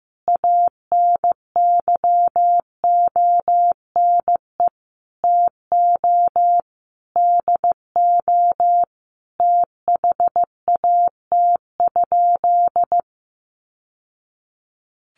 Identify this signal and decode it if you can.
Morse code, heard as a keyed tone, about 15 wpm, 700 Hz. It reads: ANYONE TO DO THAT?
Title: Sentences from Top 900 Words 15wpm